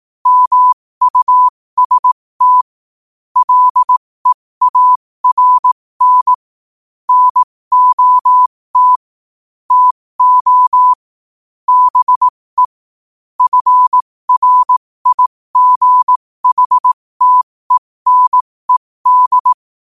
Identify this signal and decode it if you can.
Morse code, heard as a keyed tone, about 18 wpm, spaced as for 15 wpm, 1000 Hz. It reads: MUST LEARN NOT TO BE FRIGHTENED
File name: qotdfiles / 15wpm